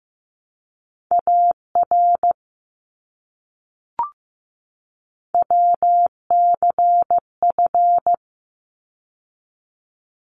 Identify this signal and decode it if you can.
Morse code, heard as a keyed tone, about 15 wpm, 700 Hz. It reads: AR WCF
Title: ARRL Sections 15wpm